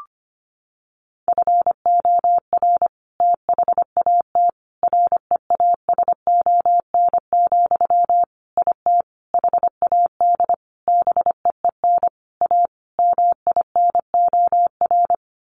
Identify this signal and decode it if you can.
Morse code, heard as a keyed tone, about 25 wpm, 700 Hz. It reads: FOR THAT REASON, IT HAD BEEN A MINOR
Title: Sentences from Top 1500 Words 25wpm